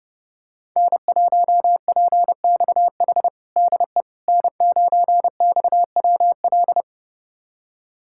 Morse code, heard as a keyed tone, about 30 wpm, 700 Hz. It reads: N1PXH DE N9XWL
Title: USA Callsign DE Callsign - 5 Characters - 30wpm